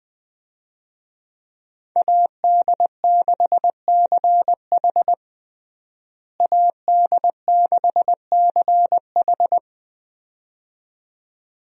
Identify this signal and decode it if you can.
Morse code, heard as a keyed tone, about 20 wpm, 700 Hz. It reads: AD6CH AD6CH
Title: USA Call Signs - 5 Characters - Repeated 2x - 20wpm